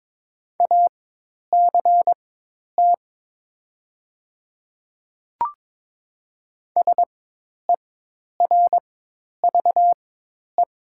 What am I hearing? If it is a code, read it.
Morse code, heard as a keyed tone, about 22 wpm, spaced as for 10 wpm, 700 Hz. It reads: ACT SERVE
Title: 4x Character Spacing - Top 500 Words - 22wpm